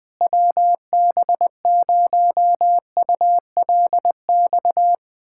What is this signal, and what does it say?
Morse code, heard as a keyed tone, about 20 wpm, 700 Hz. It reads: WB0ULX
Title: K1USN SST QSO Practice - 20wpm